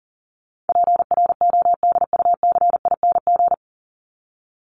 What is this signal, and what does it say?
Morse code, heard as a keyed tone, about 40 wpm, 700 Hz. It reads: PRODUCING
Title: ING Suffix - Top 100 Words with Base Words 40wpm